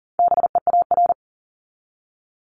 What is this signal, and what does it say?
Morse code, heard as a keyed tone, about 40 wpm, 700 Hz. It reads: BEAR
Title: ING Suffix - Top 100 Words with Base Words 40wpm